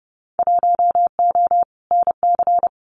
Morse code, heard as a keyed tone, about 30 wpm, 700 Hz. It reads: 1O NC
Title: Winter Field Day Exchanges 30wpm